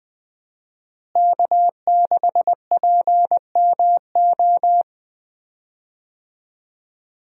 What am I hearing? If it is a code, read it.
Morse code, heard as a keyed tone, about 20 wpm, 700 Hz. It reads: K6PMO
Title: USA Call Signs - 5 Characters - Repeated 2x - 20wpm